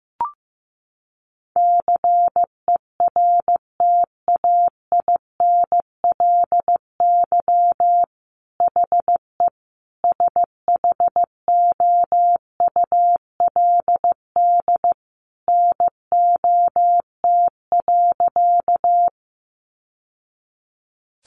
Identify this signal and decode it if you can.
Morse code, heard as a keyed tone, about 15 wpm, 700 Hz. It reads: CERTAINLY HE SHOULD NOT.
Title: Sentences from Top 900 Words 15wpm